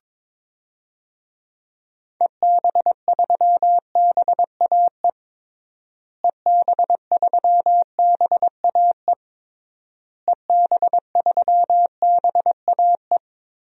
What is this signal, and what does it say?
Morse code, heard as a keyed tone, about 21 wpm, 700 Hz. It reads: EB3BAE EB3BAE EB3BAE
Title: DXCC-100 Call Signs - 6 Characters - Repeated 3x - 22wpm